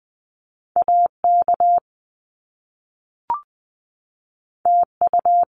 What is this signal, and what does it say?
Morse code, heard as a keyed tone, about 20 wpm, 700 Hz. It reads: AK TU
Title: Top Two Letter Combinations in Words 20wpm